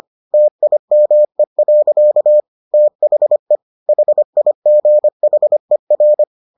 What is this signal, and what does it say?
Morse code, heard as a keyed tone, about 25 wpm, 600 Hz. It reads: TIME. THE HIGHER